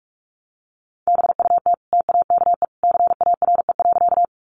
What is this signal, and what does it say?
Morse code, heard as a keyed tone, about 44 wpm, 700 Hz. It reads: BUT TAKE CARE.
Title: Sentences from Top 300 Words 45wpm